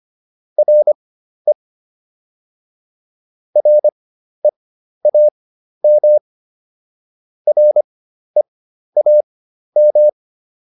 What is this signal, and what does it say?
Morse code, heard as a keyed tone, about 25 wpm, spaced as for 12 wpm, 600 Hz. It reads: RE REAM REAM